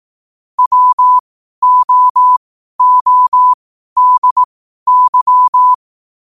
Morse code, heard as a keyed tone, about 18 wpm, spaced as for 12 wpm, 1000 Hz. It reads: WOODY
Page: New quotes every day in morse code at 12 Words per minute.